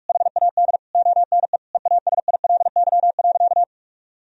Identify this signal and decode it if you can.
Morse code, heard as a keyed tone, about 46 wpm, 700 Hz. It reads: HAD ONE EASILY.
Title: Sentences from Top 1000 Words - Review Part 3 - 45wpm